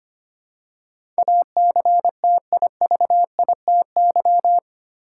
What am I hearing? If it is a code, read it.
Morse code, heard as a keyed tone, about 25 wpm, 700 Hz. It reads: ACTIVITY
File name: practice / 25wpm / 8 Letter Words - First 3 Letters Given - 25wpm